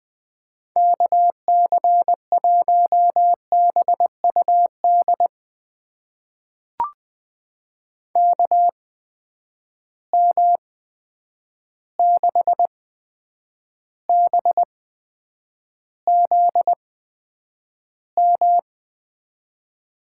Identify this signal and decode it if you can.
Morse code, heard as a keyed tone, about 21 wpm, spaced as for 9 wpm, 700 Hz. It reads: KC1BUD KM6BZM
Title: USA Call Signs - 6 Characters - 8x Character Spacing - 20wpm